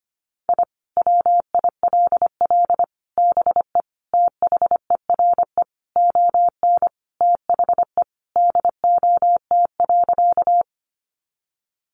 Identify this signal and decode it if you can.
Morse code, heard as a keyed tone, about 25 wpm, 700 Hz. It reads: I WILL BE THERE ON THE DOT.
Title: Sentences from Top 1500 Words 25wpm